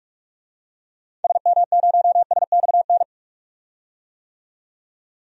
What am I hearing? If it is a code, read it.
Morse code, heard as a keyed tone, about 42 wpm, 700 Hz. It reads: SM0SXN
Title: DXCC-100 Call Signs - 6 Characters - 45wpm